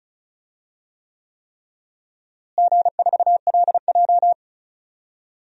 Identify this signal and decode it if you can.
Morse code, heard as a keyed tone, about 34 wpm, 700 Hz. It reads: G4LJ
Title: DXCC-100 Call Signs - 4 Characters - Repeated 3x - 35wpm